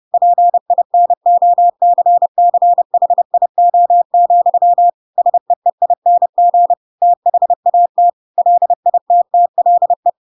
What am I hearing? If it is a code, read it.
Morse code, heard as a keyed tone, about 30 wpm, 700 Hz. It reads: PINOCCHIO, SEEING THAT LITTLE
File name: Pinocchio / Pinocchio ch7 30 30 ik1hge